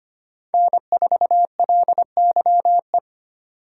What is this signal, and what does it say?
Morse code, heard as a keyed tone, about 32 wpm, 700 Hz. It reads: N4LYE